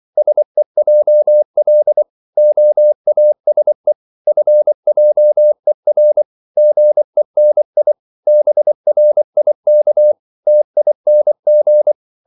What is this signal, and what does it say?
Morse code, heard as a keyed tone, about 24 wpm, 600 Hz. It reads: SEJL OASE FJER GENI BRIK TING
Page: Korte danske ord 24wpm